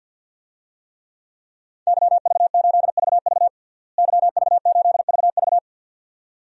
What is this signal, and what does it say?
Morse code, heard as a keyed tone, about 53 wpm, 700 Hz. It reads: YV8VV YV8VV
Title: DXCC-100 Call Signs - 5 Characters - Repeated 2x - 50wpm